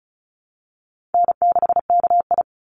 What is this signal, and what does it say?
Morse code, heard as a keyed tone, about 36 wpm, 700 Hz. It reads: N6KI
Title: Call Signs - 35wpm